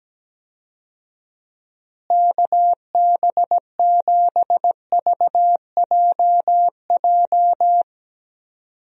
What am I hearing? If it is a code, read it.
Morse code, heard as a keyed tone, about 17 wpm, 700 Hz. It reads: KB7VJJ
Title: USA Call Signs - 6 Characters - Repeated 3x - 17wpm